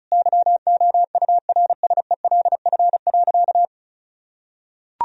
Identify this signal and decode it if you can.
Morse code, heard as a keyed tone, about 36 wpm, 700 Hz. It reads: YOURSELF.
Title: Sentences from Top 1000 Words - Review Part 2 - 35wpm